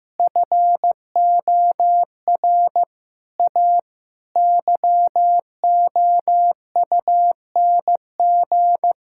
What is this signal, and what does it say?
Morse code, heard as a keyed tone, about 15 wpm, 700 Hz. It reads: FOR A YOUNG
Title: Sentences from Top 1000 Words - Review Part 5 - 15wpm